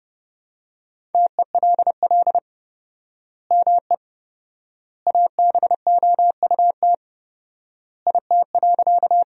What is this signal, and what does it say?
Morse code, heard as a keyed tone, about 29 wpm, 700 Hz. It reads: TELL ME ABOUT IT.
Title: 4x Word-Spacing - Sentences from Top 200 Words - 30wpm